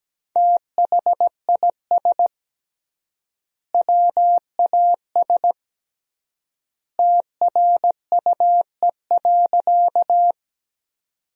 Morse code, heard as a keyed tone, about 17 wpm, 700 Hz. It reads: THIS WAS TRUE.
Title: 3x Word-Spacing - Sentences from Top 400 Words - 17wpm